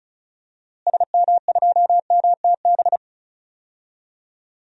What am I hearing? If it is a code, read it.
Morse code, heard as a keyed tone, about 34 wpm, 700 Hz. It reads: SM2MTB
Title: DXCC-100 Call Signs - 6 Characters - Repeated 2x - 35wpm